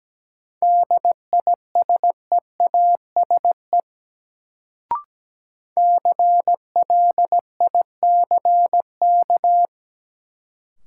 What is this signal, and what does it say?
Morse code, heard as a keyed tone, about 17 wpm, 700 Hz. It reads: DISEASE CLICK
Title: Sentences from Top 1100 Words - New Word Drill Alt - 17wpm